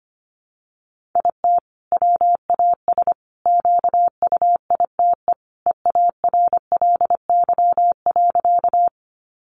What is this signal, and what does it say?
Morse code, heard as a keyed tone, about 25 wpm, 700 Hz. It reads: IT WAS QUITE EARLY.
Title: Sentences from Top 700 Words 25wpm